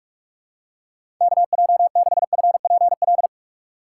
Morse code, heard as a keyed tone, about 43 wpm, 690 Hz. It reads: KJ6FPL
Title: USA Call Signs - 6 Characters - 3x Character Spacing - 45wpm